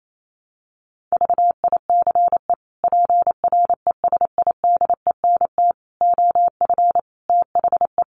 Morse code, heard as a keyed tone, about 28 wpm, 700 Hz. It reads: VICE PRESIDENT OF THE
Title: Sets of 5 Words Encore2 28wpm